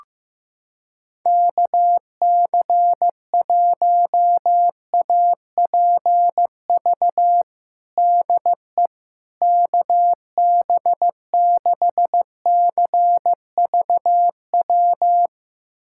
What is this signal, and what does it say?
Morse code, heard as a keyed tone, about 15 wpm, 700 Hz. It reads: KC1APV DE KB6CVW
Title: USA Callsign DE Callsign 15wpm